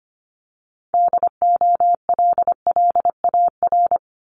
Morse code, heard as a keyed tone, about 25 wpm, 700 Hz. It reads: DOLLAR
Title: Multi-letter Phonemes 25wpm